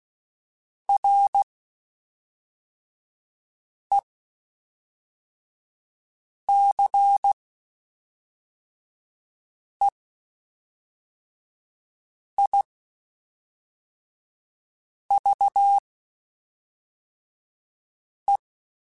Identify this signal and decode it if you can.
Morse code, heard as a keyed tone, about 16 wpm, spaced as for 3 wpm, 800 Hz. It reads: RECEIVE
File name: Quotes Spaced / Quotes Spaced-mp3s / Advice sp @5